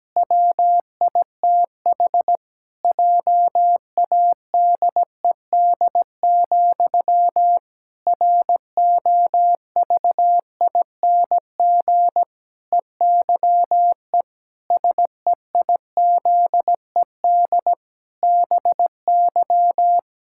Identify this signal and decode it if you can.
Morse code, heard as a keyed tone, about 17 wpm, 700 Hz. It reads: WITH JADED, ROVING EYE SEIZED BY
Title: Pangram Challenge v2 - 17wpm